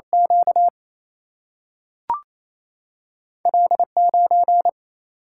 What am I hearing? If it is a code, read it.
Morse code, heard as a keyed tone, about 28 wpm, 700 Hz. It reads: Q L9
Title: Single-Letter Single-Number 28wpm